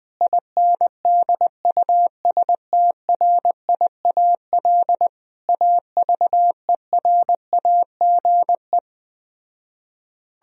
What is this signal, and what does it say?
Morse code, heard as a keyed tone, about 20 wpm, 700 Hz. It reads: INDUSTRIAL AVERAGE
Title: Speed-Racing - Sets of 5 Words 17wpm